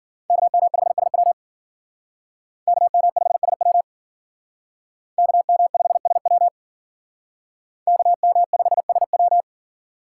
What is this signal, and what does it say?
Morse code, heard as a keyed tone, about 51 wpm, 700 Hz. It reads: KM5SW KM5SW KM5SW KM5SW